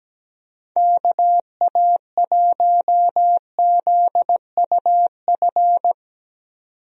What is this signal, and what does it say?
Morse code, heard as a keyed tone, about 17 wpm, 700 Hz. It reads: KA1ZUF
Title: USA Call Signs - 6 Characters - 17wpm